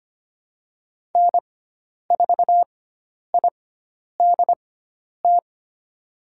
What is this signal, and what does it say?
Morse code, heard as a keyed tone, about 25 wpm, spaced as for 10 wpm, 700 Hz. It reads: N4IDT
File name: Lesson 16g - Letter c - Call Signs 25wpm